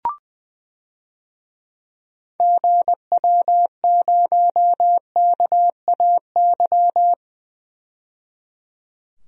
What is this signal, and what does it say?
Morse code, heard as a keyed tone, about 20 wpm, 700 Hz. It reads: GW0KAY
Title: DXCC-100 Call Signs - 6 Characters - 20wpm